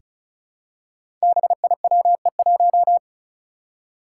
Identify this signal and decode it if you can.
Morse code, heard as a keyed tone, about 36 wpm, 700 Hz. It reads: BIWE1
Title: Five Challenging Letter-Number Combinations 35wpm